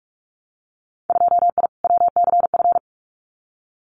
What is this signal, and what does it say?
Morse code, heard as a keyed tone, about 46 wpm, 700 Hz. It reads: 2I WCF
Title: Winter Field Day Exchanges 45wpm